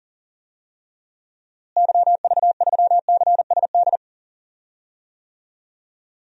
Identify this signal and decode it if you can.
Morse code, heard as a keyed tone, about 40 wpm, 700 Hz. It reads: YV3CSD
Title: DXCC-100 Call Signs - 6 Characters - Repeated 2x - 40wpm